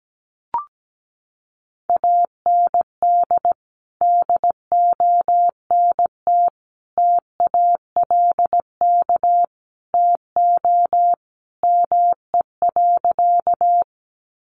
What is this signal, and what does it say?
Morse code, heard as a keyed tone, about 17 wpm, 700 Hz. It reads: AND DONT TALK TO ME.
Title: Sentences from Top 400 Words 17wpm